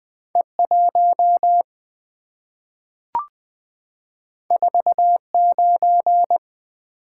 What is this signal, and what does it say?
Morse code, heard as a keyed tone, about 20 wpm, 700 Hz. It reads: E1 49
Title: Two Letter-Number Combinations - 20wpm